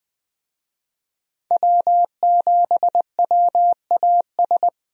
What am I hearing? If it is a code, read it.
Morse code, heard as a keyed tone, about 20 wpm, 700 Hz. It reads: W7WAS
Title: USA Call Signs - 5 Characters - Repeated 3x - 20wpm